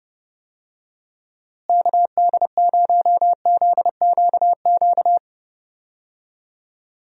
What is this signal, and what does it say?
Morse code, heard as a keyed tone, about 30 wpm, 700 Hz. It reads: KD0ZQQ
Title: USA Call Signs - 6 Characters - Repeated 2x - 30wpm